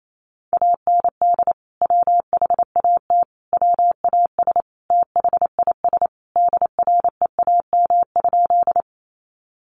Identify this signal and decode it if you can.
Morse code, heard as a keyed tone, about 28 wpm, 700 Hz. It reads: AND WHAT WAS THIS DREAM?
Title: Sentences from Top 1500 Words 28wpm